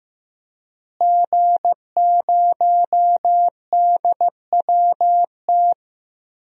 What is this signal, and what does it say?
Morse code, heard as a keyed tone, about 16 wpm, 700 Hz. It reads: G0DWT
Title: DXCC-100 Call Signs - 5 Characters - Repeated 2x - 15wpm